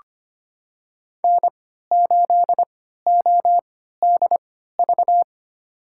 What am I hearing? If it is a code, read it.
Morse code, heard as a keyed tone, about 25 wpm, spaced as for 14 wpm, 700 Hz. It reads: N8ODV
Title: USA Call Signs - 5 Characters - 3x Character Spacing - 25wpm